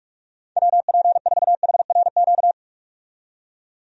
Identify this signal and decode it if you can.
Morse code, heard as a keyed tone, about 46 wpm, 700 Hz. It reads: WP4HRQ
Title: USA Call Signs - 6 Characters - 45wpm